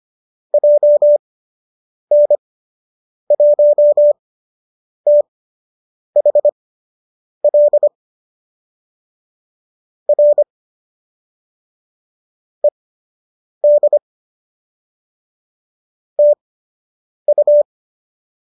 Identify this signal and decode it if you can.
Morse code, heard as a keyed tone, about 25 wpm, spaced as for 8 wpm, 590 Hz. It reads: JN1THL R ED TU
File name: qso003 08